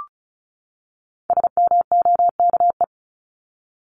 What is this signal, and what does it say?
Morse code, heard as a keyed tone, about 34 wpm, 700 Hz. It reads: SMOKE
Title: Top 1501-1750 Words 35wpm